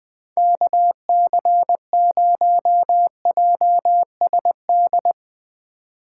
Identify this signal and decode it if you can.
Morse code, heard as a keyed tone, about 21 wpm, 700 Hz. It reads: KC0JSD
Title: Speed Racing - USA Call Signs 20wpm